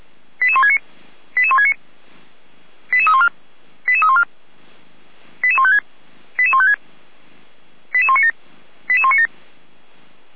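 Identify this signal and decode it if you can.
Five-tone selective call (ZVEI), heard as quick runs of five tones, tones of about 70 ms. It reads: 88169 88169 88314 88314 88156 88156 88197 88197
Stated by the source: ZVEI five-tone selective call received by SDR, logged by Feuerwehr Orlamünde fire brigade.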